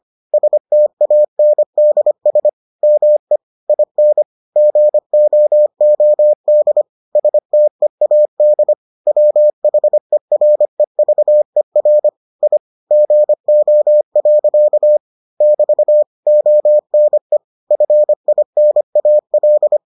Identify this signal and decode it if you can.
Morse code, heard as a keyed tone, about 25 wpm, 600 Hz. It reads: STANDS ME IN GOOD STEAD WHEREVER I GO. <BT> ONE FINAL